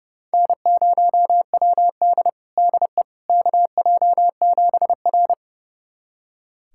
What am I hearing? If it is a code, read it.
Morse code, heard as a keyed tone, about 30 wpm, 700 Hz. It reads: N0WD DE KJ7R
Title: USA Callsign DE Callsign - 4 Characters - 30wpm